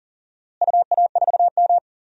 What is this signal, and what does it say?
Morse code, heard as a keyed tone, about 37 wpm, 700 Hz. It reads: UA4M
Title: DXCC-100 Call Signs - 4 Characters - Repeated 2x - 40wpm